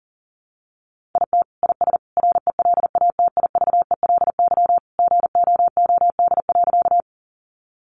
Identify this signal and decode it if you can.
Morse code, heard as a keyed tone, about 40 wpm, 700 Hz. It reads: IT IS RELATIVELY GOOD.
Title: Sentences from Top 1100 Words 40wpm